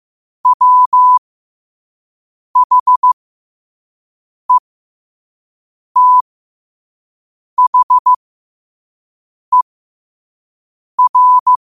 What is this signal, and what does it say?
Morse code, heard as a keyed tone, about 15 wpm, spaced as for 5 wpm, 1000 Hz. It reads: WHETHER
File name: qotdfiles / 5wpm